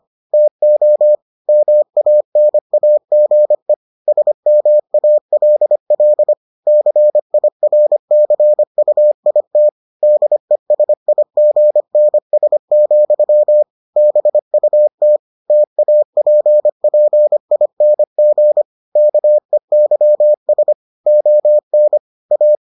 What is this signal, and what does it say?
Morse code, heard as a keyed tone, about 25 wpm, 600 Hz. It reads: TO MANAGE SMALL CIRCUIT DESIGNS, BUT TAPPING KEYS ON A